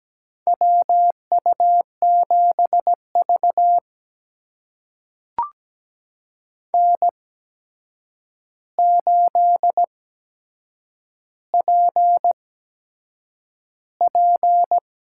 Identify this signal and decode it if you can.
Morse code, heard as a keyed tone, about 17 wpm, spaced as for 7 wpm, 700 Hz. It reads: WU7V N8PP
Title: USA Call Signs - 4 Characters - 8x Character Spacing - 17wpm